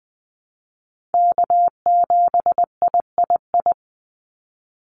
Morse code, heard as a keyed tone, about 20 wpm, 700 Hz. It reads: K7III
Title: NA SOTA Chaser Callsigns 20wpm